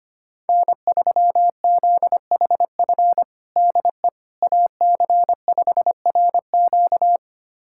Morse code, heard as a keyed tone, about 25 wpm, 700 Hz. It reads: N3ZHF DE AC5RQ
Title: USA Callsign DE Callsign - 5 Characters - 25wpm